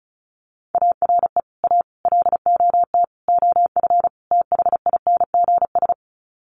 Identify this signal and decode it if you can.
Morse code, heard as a keyed tone, about 34 wpm, 700 Hz. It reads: ARE A LOT OF THINGS
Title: Sets of 5 Words Encore2 35wpm